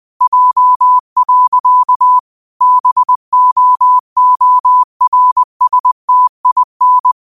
Morse code, heard as a keyed tone, about 20 wpm, 1000 Hz. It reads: J. BOORSTIN
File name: qotdfiles / 20wpm